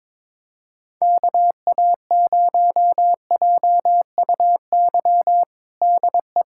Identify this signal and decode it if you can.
Morse code, heard as a keyed tone, about 22 wpm, 700 Hz. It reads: KA0JUY DE
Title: USA Callsign DE Callsign 22wpm